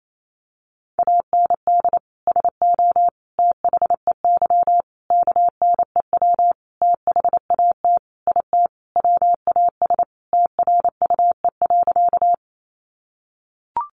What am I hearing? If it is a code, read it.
Morse code, heard as a keyed tone, about 28 wpm, 700 Hz. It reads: AND SO THEY KNEW THAT IT WAS TRUE.
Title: Sentences from Top 400 Words 28wpm